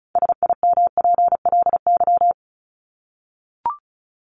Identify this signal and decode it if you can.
Morse code, heard as a keyed tone, about 34 wpm, 700 Hz. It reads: SIMPLY
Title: Top 501-750 Words 35wpm